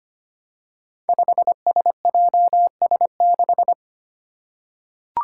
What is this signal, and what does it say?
Morse code, heard as a keyed tone, about 25 wpm, 700 Hz. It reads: 5SJS6
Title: Five Challenging Letter-Number Combinations 25wpm